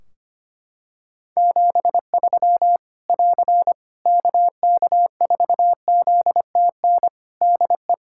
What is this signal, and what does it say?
Morse code, heard as a keyed tone, about 25 wpm, 700 Hz. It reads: 73 <AR> KK4ZTN DE
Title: Practice QSO - Exchange 4 - Ending - 25wpm